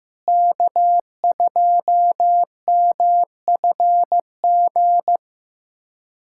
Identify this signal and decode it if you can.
Morse code, heard as a keyed tone, about 15 wpm, 700 Hz. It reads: K2MFG
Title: USA Callsign DE Callsign - 5 Characters - 15wpm